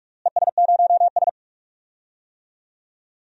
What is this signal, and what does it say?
Morse code, heard as a keyed tone, about 41 wpm, 700 Hz. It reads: ES0S
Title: DXCC-100 Call Signs - 4 Characters - Repeated 3x - 45wpm